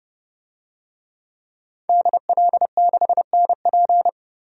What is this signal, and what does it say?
Morse code, heard as a keyed tone, about 30 wpm, 700 Hz. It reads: DL6NP
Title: DXCC-100 Call Signs - 5 Characters - 30wpm